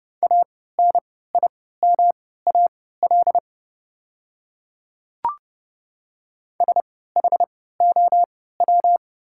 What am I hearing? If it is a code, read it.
Morse code, heard as a keyed tone, about 30 wpm, spaced as for 17 wpm, 700 Hz. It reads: ANIMAL SHOW
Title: 3x Character Spacing - Top 200 Words - 30wpm